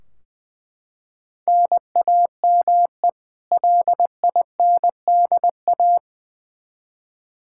Morse code, heard as a keyed tone, about 20 wpm, 700 Hz. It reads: NAME LINDA
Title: Practice QSO - Exchange 1 - Beginning - 20wpm